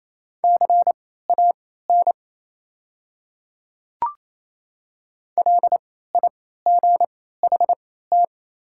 Morse code, heard as a keyed tone, about 28 wpm, spaced as for 16 wpm, 690 Hz. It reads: CAN LIGHT
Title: 3x Character Spacing - Top 200 Words - 28wpm